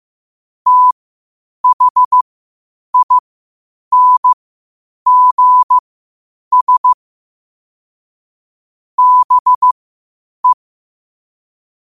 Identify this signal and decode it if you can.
Morse code, heard as a keyed tone, about 15 wpm, spaced as for 8 wpm, 1000 Hz. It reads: THINGS BE
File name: qotdfiles / 8wpm